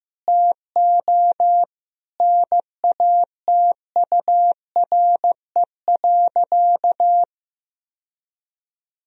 Morse code, heard as a keyed tone, about 15 wpm, 700 Hz. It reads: TO NATURE.
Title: Sentences from Top 1000 Words - Review Part 4 - 15wpm